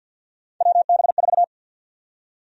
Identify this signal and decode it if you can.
Morse code, heard as a keyed tone, about 52 wpm, 700 Hz. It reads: WB4
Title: USA Call Sign Prefixes - 50wpm